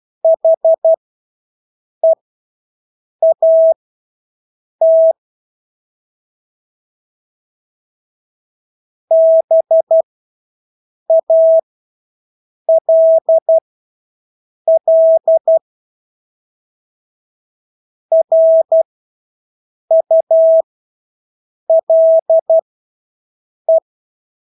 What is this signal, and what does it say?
Morse code, heard as a keyed tone, about 12 wpm, spaced as for 6 wpm, 650 Hz. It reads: HEAT BALL RULE
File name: Engelske ord 12-6 wpm